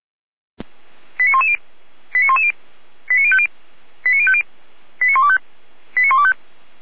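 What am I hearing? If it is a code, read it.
Five-tone selective call (ZVEI), heard as quick runs of five tones, tones of about 70 ms. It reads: 89110 89110 89050 89050 89125 89125
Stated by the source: Five-tone ZVEI selective call received by SDR.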